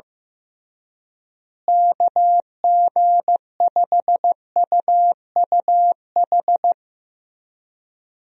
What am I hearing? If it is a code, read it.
Morse code, heard as a keyed tone, about 15 wpm, 700 Hz. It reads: KG5UUH
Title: USA Call Signs - 6 Characters - Repeated 3x - 15wpm